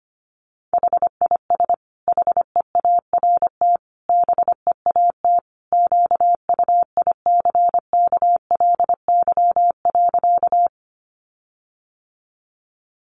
Morse code, heard as a keyed tone, about 25 wpm, 700 Hz. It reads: HIS HEART BEAT QUICKLY.
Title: Sentences from Top 1300 Words 25wpm